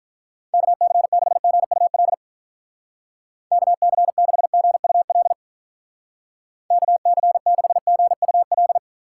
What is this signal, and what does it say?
Morse code, heard as a keyed tone, about 46 wpm, 700 Hz. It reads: KC6GUL KC6GUL KC6GUL